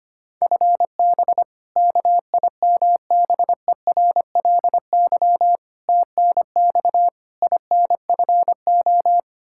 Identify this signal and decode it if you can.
Morse code, heard as a keyed tone, about 25 wpm, 700 Hz. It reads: FB KIMBERLY TNX INFO
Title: Practice QSO - Exchange 3 - More Info - 25wpm